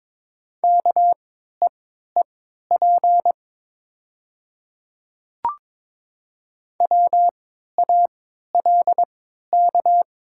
Morse code, heard as a keyed tone, about 22 wpm, spaced as for 12 wpm, 700 Hz. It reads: KEEP WALK
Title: 3x Character Spacing - Top 500 Words - 22wpm